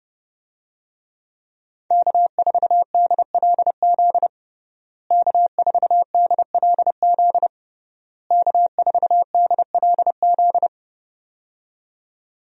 Morse code, heard as a keyed tone, about 30 wpm, 700 Hz. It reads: K4DLZ K4DLZ K4DLZ
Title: USA Call Signs - 5 Characters - Repeated 3x - 30wpm